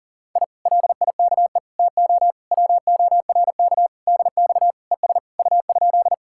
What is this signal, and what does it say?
Morse code, heard as a keyed tone, about 50 wpm, 700 Hz. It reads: I LIKE TO WORK DX ES U?